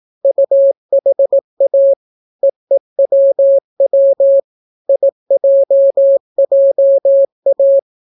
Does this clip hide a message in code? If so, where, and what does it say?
Yes: Morse code from 0.2 to 7.8 s, heard as a keyed tone, about 18 wpm, 550 Hz. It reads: UHA EEWW IJJA